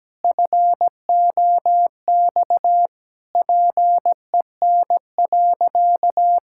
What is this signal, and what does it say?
Morse code, heard as a keyed tone, about 17 wpm, 700 Hz. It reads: FOX PEN.
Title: Pangram Challenge v2 - 17wpm